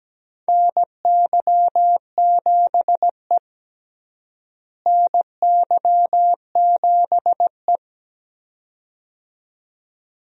Morse code, heard as a keyed tone, about 17 wpm, 700 Hz. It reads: NY7E NY7E
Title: USA Call Signs - 4 Characters - Repeated 2x - 17wpm